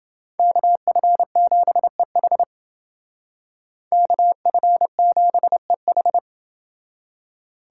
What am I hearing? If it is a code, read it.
Morse code, heard as a keyed tone, about 29 wpm, 700 Hz. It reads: KF7EH KF7EH